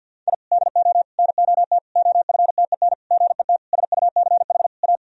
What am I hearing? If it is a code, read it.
Morse code, heard as a keyed tone, about 52 wpm, 700 Hz. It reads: I DO NOT OFTEN GET SUCH A
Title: Sentences from Top 1000 Words - Review Part 1 - 50wpm